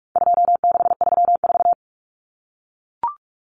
Morse code, heard as a keyed tone, about 44 wpm, 700 Hz. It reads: 2634
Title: Numbers - 4-Digit 45wpm